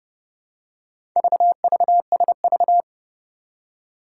Morse code, heard as a keyed tone, about 30 wpm, 700 Hz. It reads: VVSV
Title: Four Challenging Letter-Number Combinations 30wpm